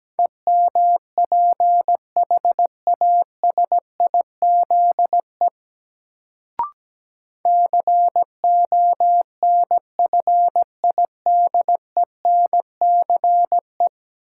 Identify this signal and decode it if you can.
Morse code, heard as a keyed tone, about 17 wpm, 700 Hz. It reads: EMPHASIZE CONFIDENCE
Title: Sentences from Top 1500 Words - New Word Drill - 17wpm